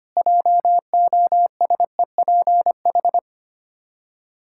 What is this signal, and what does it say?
Morse code, heard as a keyed tone, about 25 wpm, 700 Hz. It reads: JOSEPH
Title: K1USN SST QSO Practice 25wpm